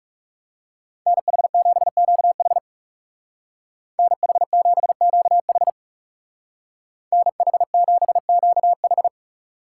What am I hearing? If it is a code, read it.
Morse code, heard as a keyed tone, about 35 wpm, 700 Hz. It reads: NH7QH NH7QH NH7QH
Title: Speed Racing - USA 2x2 Call Signs 30wpm